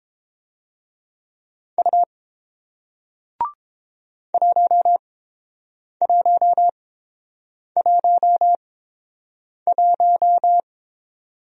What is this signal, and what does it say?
Morse code, heard as a keyed tone, about 30 wpm, spaced as for 7 wpm, 700 Hz. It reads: U 1111